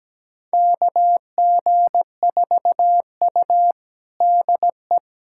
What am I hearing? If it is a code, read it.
Morse code, heard as a keyed tone, about 17 wpm, 700 Hz. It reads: KG4U DE
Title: USA Callsign DE Callsign - 4 Characters - 17wpm